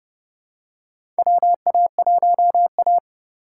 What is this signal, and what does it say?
Morse code, heard as a keyed tone, about 32 wpm, 700 Hz. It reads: WA1A
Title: USA Call Signs - 4 Characters - 4x Character Spacing - 30wpm